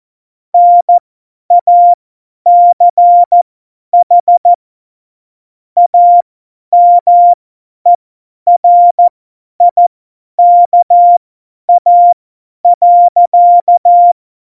Morse code, heard as a keyed tone, about 14 wpm, spaced as for 10 wpm, 700 Hz. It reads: NACH AMERIKA.
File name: Morsekurs / Witze / Witz 05. Januar 2021, 50 F70